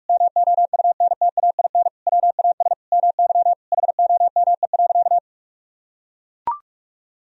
Morse code, heard as a keyed tone, about 43 wpm, 700 Hz. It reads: MOUNTAIN WAS MY HOME.
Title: Sentences from Top 1000 Words - Review Part 1 - 45wpm